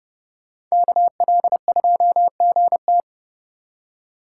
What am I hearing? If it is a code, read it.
Morse code, heard as a keyed tone, about 38 wpm, 700 Hz. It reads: KL2GT